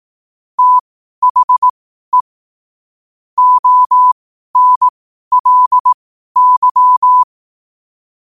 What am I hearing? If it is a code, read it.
Morse code, heard as a keyed tone, about 18 wpm, spaced as for 12 wpm, 1000 Hz. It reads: THE ONLY
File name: qotdfiles / 12wpm